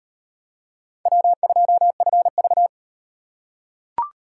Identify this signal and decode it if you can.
Morse code, heard as a keyed tone, about 38 wpm, 700 Hz. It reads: W2FV